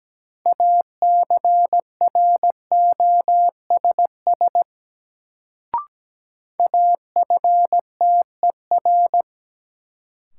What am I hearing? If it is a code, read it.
Morse code, heard as a keyed tone, about 17 wpm, 700 Hz. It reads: ACROSS AFTER
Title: Sets of 2 Words - New Word Drill - 17wpm